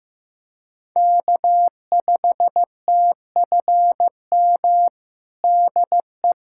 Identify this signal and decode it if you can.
Morse code, heard as a keyed tone, about 16 wpm, 700 Hz. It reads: K5TFM DE
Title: USA Callsign DE Callsign 15wpm